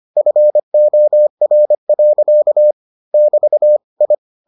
Morse code, heard as a keyed tone, about 25 wpm, 600 Hz. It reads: FOR. <BT> I